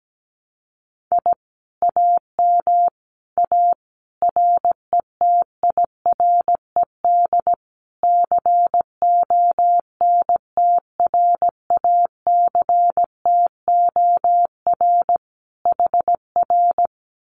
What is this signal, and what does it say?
Morse code, heard as a keyed tone, about 17 wpm, 700 Hz. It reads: I AM A RETIRED CONTRACTOR HR
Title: Common QSO Phrases Encore 17wpm